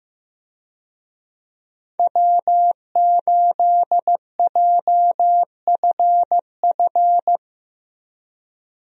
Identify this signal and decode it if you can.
Morse code, heard as a keyed tone, about 15 wpm, 700 Hz. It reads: W8JFF
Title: USA Call Signs - 5 Characters - Repeated 3x - 15wpm